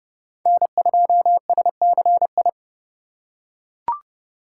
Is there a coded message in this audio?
Yes: Morse code from 0.5 to 2.5 s, heard as a keyed tone, about 29 wpm, 700 Hz. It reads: N2SCI